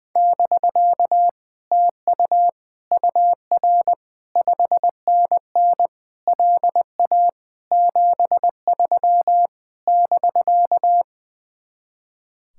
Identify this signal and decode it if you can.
Morse code, heard as a keyed tone, about 21 wpm, 700 Hz. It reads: <BK> TU UR 5NN LA 73 <BK>
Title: Speed-Racing Common QSO Phrases - 20wpm